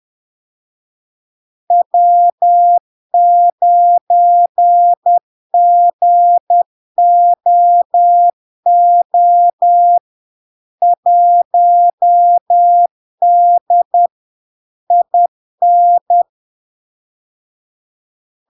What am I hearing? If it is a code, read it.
Morse code, heard as a keyed tone, about 10 wpm, 700 Hz. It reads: W9GOO 1D IN
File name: LARK-FD-2022-Log-in-CW-10wpm-700-Hz